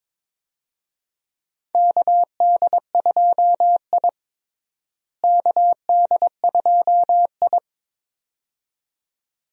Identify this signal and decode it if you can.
Morse code, heard as a keyed tone, about 22 wpm, 700 Hz. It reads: KD2I KD2I
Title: USA Call Signs - 4 Characters - Repeated 2x - 22wpm